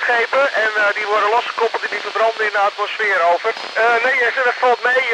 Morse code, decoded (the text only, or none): none